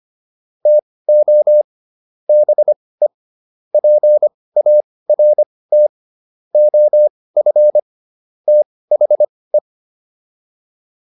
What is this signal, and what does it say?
Morse code, heard as a keyed tone, about 25 wpm, spaced as for 18 wpm, 600 Hz. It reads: TO BE PART OF THE